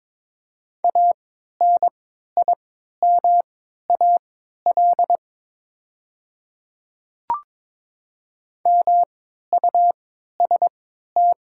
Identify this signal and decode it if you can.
Morse code, heard as a keyed tone, about 22 wpm, spaced as for 13 wpm, 700 Hz. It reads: ANIMAL MUST
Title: 3x Character Spacing - Top 200 Words - 22wpm